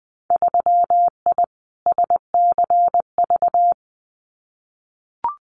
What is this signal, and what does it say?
Morse code, heard as a keyed tone, about 20 wpm, 700 Hz. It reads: 3I SCV
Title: Winter Field Day Exchanges Encore 20wpm